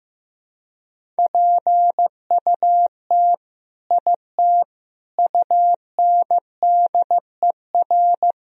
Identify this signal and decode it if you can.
Morse code, heard as a keyed tone, about 15 wpm, 700 Hz. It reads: PUT IT UNDER
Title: Sentences from Top 1000 Words - Review Part 1 - 15wpm